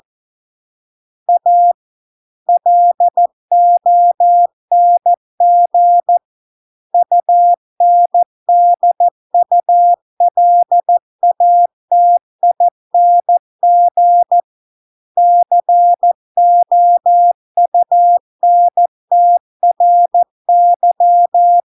Morse code, heard as a keyed tone, about 14 wpm, 700 Hz. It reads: A LONG UNDULATING COUNTRY